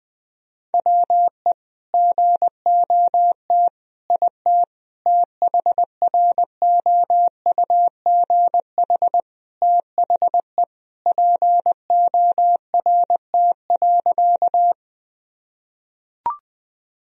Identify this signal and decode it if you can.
Morse code, heard as a keyed tone, about 21 wpm, 700 Hz. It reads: WE GOT IT THROUGH THE PORT.
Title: Sentences from Top 1000 Words - Review Part 1 - 20wpm